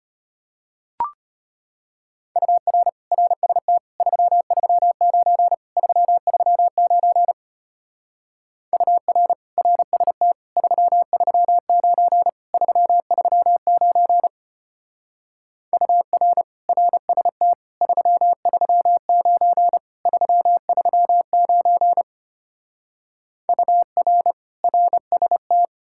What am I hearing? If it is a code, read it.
Morse code, heard as a keyed tone, about 32 wpm, 700 Hz. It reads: UR RST 339 339 UR RST 339 339 UR RST 339 339 UR RST